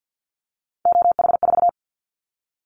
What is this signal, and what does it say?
Morse code, heard as a keyed tone, about 50 wpm, 700 Hz. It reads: OH4
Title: Callsign Prefixes for Top 25 DXCC - 50wpm